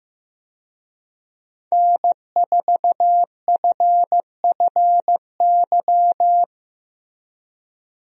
Morse code, heard as a keyed tone, about 15 wpm, 700 Hz. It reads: N4FFY
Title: USA Call Signs - 5 Characters - Repeated 3x - 15wpm